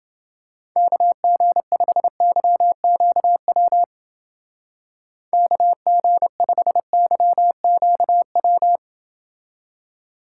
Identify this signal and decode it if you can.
Morse code, heard as a keyed tone, about 29 wpm, 700 Hz. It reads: KG5YQW KG5YQW